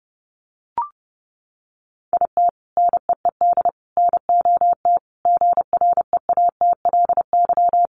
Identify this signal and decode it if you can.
Morse code, heard as a keyed tone, about 28 wpm, 700 Hz. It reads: IT NEED NOT GREATLY
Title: Sentences from Top 1400 Words 30wpm